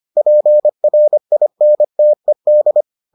Morse code, heard as a keyed tone, about 25 wpm, 600 Hz. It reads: PRINTED